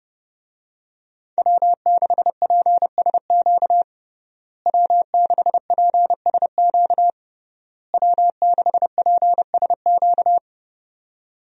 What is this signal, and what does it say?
Morse code, heard as a keyed tone, about 30 wpm, 700 Hz. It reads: W6PSQ W6PSQ W6PSQ
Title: USA Call Signs - 5 Characters - Repeated 3x - 30wpm